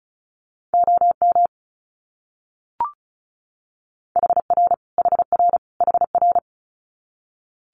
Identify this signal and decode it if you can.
Morse code, heard as a keyed tone, about 34 wpm, 700 Hz. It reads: OM HR HR HR
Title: Abbreviations Repeated 3x - 35wpm